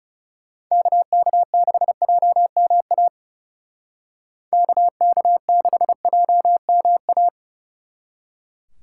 Morse code, heard as a keyed tone, about 37 wpm, 700 Hz. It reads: KK6JMA KK6JMA
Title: Speed Racing - USA 2x3 Call Signs 30wpm